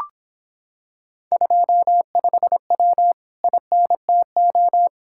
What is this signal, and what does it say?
Morse code, heard as a keyed tone, about 26 wpm, 700 Hz. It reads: 25W INTO